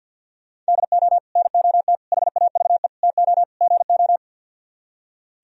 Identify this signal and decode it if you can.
Morse code, heard as a keyed tone, about 58 wpm, 700 Hz. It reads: DO NOT HAVE TO GO
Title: Speed-Racing - Sets of 5 Words 50wpm